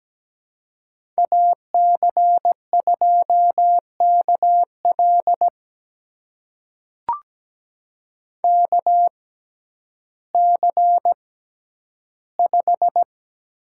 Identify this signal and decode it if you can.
Morse code, heard as a keyed tone, about 17 wpm, spaced as for 10 wpm, 700 Hz. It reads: AC2KL KC5
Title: USA Call Signs - 5 Characters - 6x Character Spacing - 17wpm